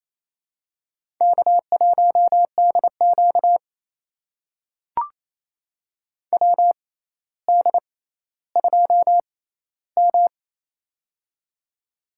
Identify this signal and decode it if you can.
Morse code, heard as a keyed tone, about 28 wpm, spaced as for 14 wpm, 700 Hz. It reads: K1DQ WD2M
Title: USA Call Signs - 4 Characters - 6x Character Spacing - 28wpm